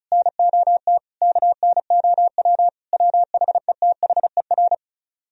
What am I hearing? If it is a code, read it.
Morse code, heard as a keyed tone, about 46 wpm, 700 Hz. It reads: NOT KNOW WHETHER